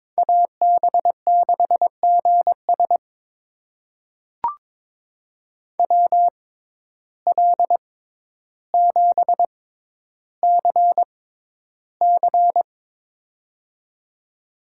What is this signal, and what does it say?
Morse code, heard as a keyed tone, about 22 wpm, spaced as for 11 wpm, 700 Hz. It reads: AB6GS WL7CC
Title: USA Call Signs - 6x Character Spacing - 22wpm